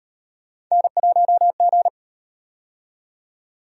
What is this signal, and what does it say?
Morse code, heard as a keyed tone, about 38 wpm, 700 Hz. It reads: N1G